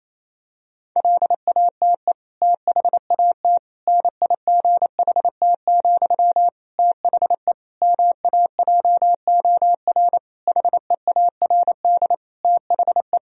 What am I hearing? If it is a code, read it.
Morse code, heard as a keyed tone, about 28 wpm, 700 Hz. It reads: LATE THAT NIGHT, THE MAJOR HEARD THE
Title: Sentences from Top 1000 Words - Review Part 4 - 28wpm